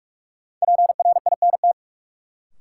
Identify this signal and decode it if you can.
Morse code, heard as a keyed tone, about 41 wpm, 700 Hz. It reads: PRINT
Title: Sentences from Top 1400 Words - New Word Drill - 45wpm